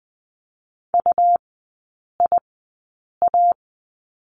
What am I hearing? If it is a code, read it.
Morse code, heard as a keyed tone, about 20 wpm, spaced as for 8 wpm, 700 Hz. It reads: UIA
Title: 172 Lesson 28e - Number 9 - Three Characters 20wpm